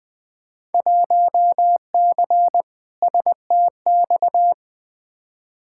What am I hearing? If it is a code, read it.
Morse code, heard as a keyed tone, about 20 wpm, 700 Hz. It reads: 1C STX
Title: Field Day Exchanges Encore - 20wpm